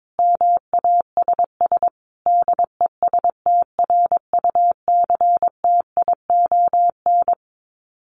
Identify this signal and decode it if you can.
Morse code, heard as a keyed tone, about 22 wpm, 700 Hz. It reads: MASS DESTRUCTION
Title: Sets of 5 Words Encore 22wpm